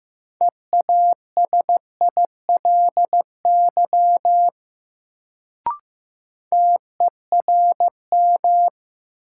Morse code, heard as a keyed tone, about 16 wpm, 700 Hz. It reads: EASILY TERM
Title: Sentences from Top 900 Words - New Word Drill - 15wpm